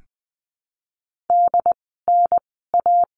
Morse code, heard as a keyed tone, about 20 wpm, spaced as for 14 wpm, 700 Hz. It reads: DNA
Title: 068 Lesson 12f - Letter d - Words 20wpm